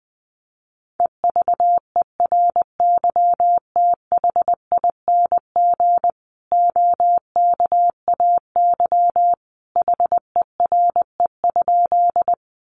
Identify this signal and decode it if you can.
Morse code, heard as a keyed tone, about 20 wpm, 700 Hz. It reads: EVERYTHING OKAY HERE?
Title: A Sound Of Thunder 20wpm